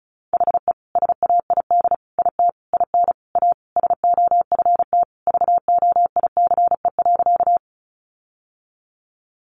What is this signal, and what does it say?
Morse code, heard as a keyed tone, about 34 wpm, 700 Hz. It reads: HE SAID IT IN A SOFT VOICE.
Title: Sentences from Top 1500 Words 35wpm